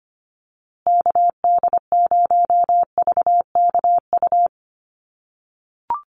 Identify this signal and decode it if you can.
Morse code, heard as a keyed tone, about 25 wpm, 700 Hz. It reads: KD0VKU
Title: USA Call Signs Repeated 2x 25wpm